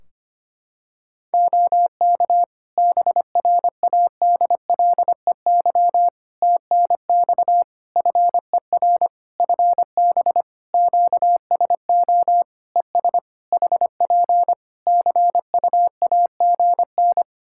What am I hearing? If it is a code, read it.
Morse code, heard as a keyed tone, about 25 wpm, 700 Hz. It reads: OK BRADLEY TNX FER FB QSO ES HP CUAGN
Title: Practice QSO - Exchange 4 - Ending - 25wpm